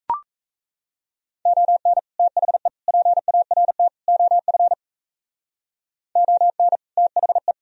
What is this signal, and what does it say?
Morse code, heard as a keyed tone, about 34 wpm, 700 Hz. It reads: ON THE PART OF ON THE
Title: Speed-Racing - Sets of 4 Words 28wpm